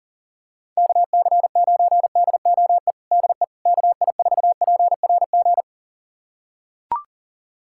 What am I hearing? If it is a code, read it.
Morse code, heard as a keyed tone, about 40 wpm, 700 Hz. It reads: KC9DOE DE KI4PRG
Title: USA Callsign DE Callsign - 6 Characters - 40wpm